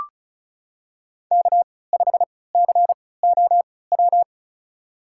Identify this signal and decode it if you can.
Morse code, heard as a keyed tone, about 36 wpm, spaced as for 20 wpm, 700 Hz. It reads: K5COW
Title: USA Call Signs - 5 Characters - 3x Character Spacing - 35wpm